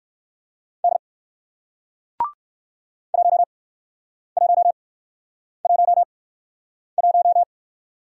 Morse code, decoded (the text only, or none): D 1111